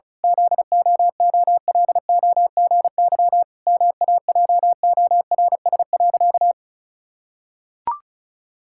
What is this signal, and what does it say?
Morse code, heard as a keyed tone, about 36 wpm, 700 Hz. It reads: ZOOLOGY MAJORS.
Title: Pangram Challenge v2 - 35wpm